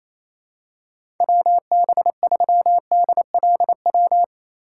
Morse code, heard as a keyed tone, about 28 wpm, 700 Hz. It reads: WB3DLW
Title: USA Call Signs - 6 Characters - 4x Character Spacing - 28wpm